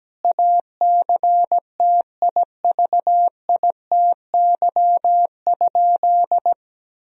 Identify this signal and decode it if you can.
Morse code, heard as a keyed tone, about 17 wpm, 700 Hz. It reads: ACTIVITY?
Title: Sentences from Top 1000 Words - Review Part 4 - 17wpm